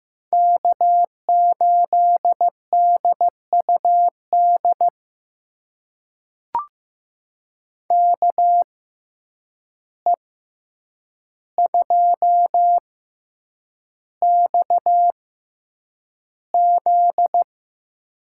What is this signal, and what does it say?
Morse code, heard as a keyed tone, about 15 wpm, spaced as for 8 wpm, 700 Hz. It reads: K8DUD KE2XZ
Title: USA Call Signs - 5 Characters - 6x Character Spacing - 15wpm